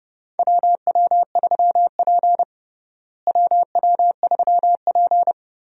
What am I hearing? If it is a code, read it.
Morse code, heard as a keyed tone, about 30 wpm, 700 Hz. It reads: WW3P WW3P
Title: USA Call Signs - 4 Characters - Repeated 3x - 30wpm